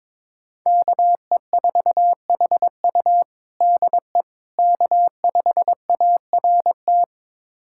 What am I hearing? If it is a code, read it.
Morse code, heard as a keyed tone, about 22 wpm, 700 Hz. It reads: KE4HU DE K5ART
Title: USA Callsign DE Callsign - 5 Characters - 22wpm